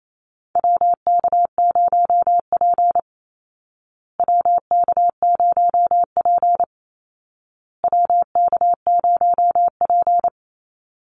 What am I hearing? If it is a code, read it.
Morse code, heard as a keyed tone, about 28 wpm, 700 Hz. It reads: WK0P WK0P WK0P
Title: Call Signs Repeated 2x - 28wpm